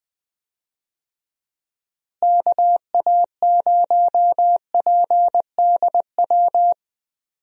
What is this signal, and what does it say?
Morse code, heard as a keyed tone, about 20 wpm, 700 Hz. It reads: KA0PDW
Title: USA Call Signs - 6 Characters - Repeated 3x - 20wpm